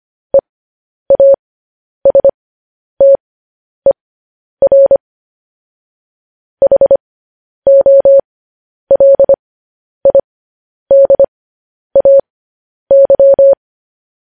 Morse code, heard as a keyed tone, about 25 wpm, spaced as for 10 wpm, 550 Hz. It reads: EASTER HOLIDAY
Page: news headlines in morse code at 10 wpm